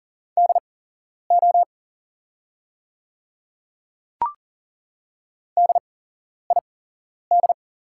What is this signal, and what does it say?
Morse code, heard as a keyed tone, about 40 wpm, spaced as for 11 wpm, 700 Hz. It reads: DO DID
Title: 8x Character Spacing - Top 300 Words - 40wpm